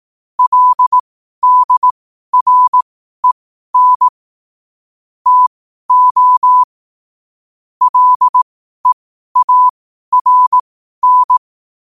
Morse code, heard as a keyed tone, about 18 wpm, spaced as for 12 wpm, 1000 Hz. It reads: LDREN TO LEARN
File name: qotdfiles / 12wpm